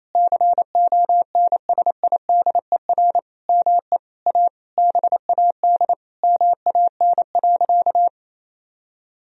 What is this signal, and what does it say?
Morse code, heard as a keyed tone, about 28 wpm, 700 Hz. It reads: CONSIDER ME A BAD MAN.
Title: Sentences from Top 1000 Words - Review Part 4 - 28wpm